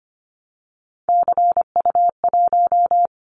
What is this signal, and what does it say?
Morse code, heard as a keyed tone, about 25 wpm, 700 Hz. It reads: CU1
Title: Callsign Prefixes for Top 50 DXCC - 25wpm